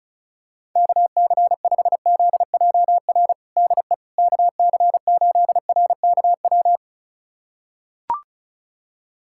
Morse code, heard as a keyed tone, about 36 wpm, 700 Hz. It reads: KC5ZJR DE KC8RKW
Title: USA Callsign DE Callsign - 6 Characters - 35wpm